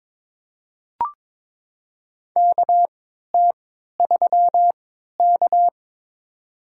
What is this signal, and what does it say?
Morse code, heard as a keyed tone, about 22 wpm, spaced as for 13 wpm, 700 Hz. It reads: KT3K
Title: USA Call Signs - 4 Characters - 3x Character Spacing - 22wpm